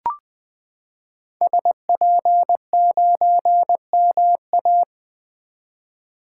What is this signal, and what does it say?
Morse code, heard as a keyed tone, about 21 wpm, 700 Hz. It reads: SP9MA
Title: 2023 SOTA Activator Call Signs - 20wpm